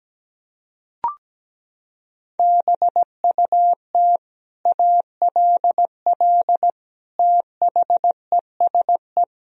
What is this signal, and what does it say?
Morse code, heard as a keyed tone, about 17 wpm, 700 Hz. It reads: BUT ALL THESE
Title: Sentences from Top 1000 Words - Review Part 3 - 17wpm